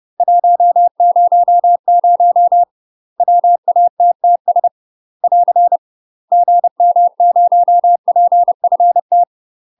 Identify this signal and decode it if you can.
Morse code, heard as a keyed tone, about 30 wpm, 700 Hz. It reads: 100 WATTS <AR> GM0PFT
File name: UK MASTER QSOs TRAINING / 30WPM / UK MASTER QSOs 0030